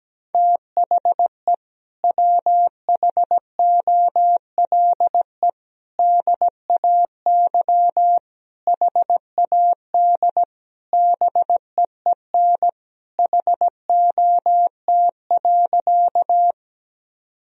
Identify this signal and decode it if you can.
Morse code, heard as a keyed tone, about 17 wpm, 700 Hz. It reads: THE WHOLE DAY HAD BEEN HOT.
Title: Sentences from Top 1000 Words - Review Part 3 - 17wpm